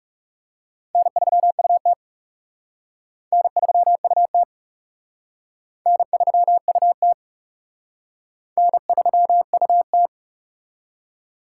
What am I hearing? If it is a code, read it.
Morse code, heard as a keyed tone, about 38 wpm, 700 Hz. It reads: N3UT N3UT N3UT N3UT